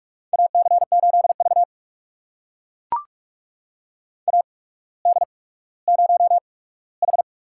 Morse code, heard as a keyed tone, about 44 wpm, spaced as for 19 wpm, 700 Hz. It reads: AC8V AD0H
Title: USA Call Signs - 8x Character Spacing - 45wpm